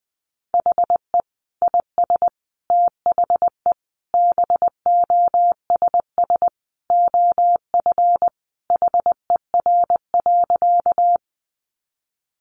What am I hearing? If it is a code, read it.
Morse code, heard as a keyed tone, about 20 wpm, 700 Hz. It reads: HE IS THE BOSS OF HER.
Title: Sentences from Top 700 Words 20wpm